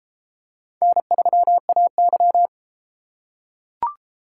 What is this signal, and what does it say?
Morse code, heard as a keyed tone, about 31 wpm, 700 Hz. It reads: N3AY